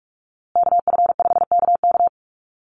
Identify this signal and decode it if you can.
Morse code, heard as a keyed tone, about 44 wpm, 700 Hz. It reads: KF5KK
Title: USA Call Signs Repeated 3x 45wpm